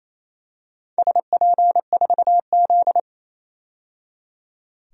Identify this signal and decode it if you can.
Morse code, heard as a keyed tone, about 29 wpm, 690 Hz. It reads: SP4Z
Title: DXCC-100 Call Signs - 4 Characters - 28wpm